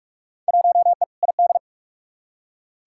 Morse code, heard as a keyed tone, about 46 wpm, 700 Hz. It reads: 1E ID
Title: Field Day Exchanges - 45wpm